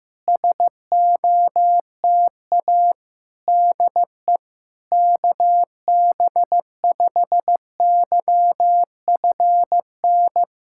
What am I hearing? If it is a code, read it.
Morse code, heard as a keyed tone, about 15 wpm, 700 Hz. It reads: SOTA DE KB5YFN
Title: SOTA QSO Practice 15wpm